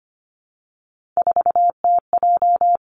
Morse code, heard as a keyed tone, about 25 wpm, 700 Hz. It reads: 4TJ
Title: Three Challenging Letter-Number Combinations 25wpm